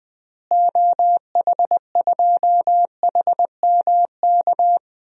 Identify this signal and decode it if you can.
Morse code, heard as a keyed tone, about 20 wpm, 700 Hz. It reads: OH2HMK
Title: DXCC-100 Call Signs - 6 Characters - Repeated 3x - 20wpm